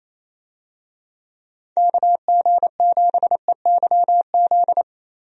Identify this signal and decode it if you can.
Morse code, heard as a keyed tone, about 27 wpm, 700 Hz. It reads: KG7EYZ
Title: USA Call Signs - 6 Characters - Repeated 3x - 28wpm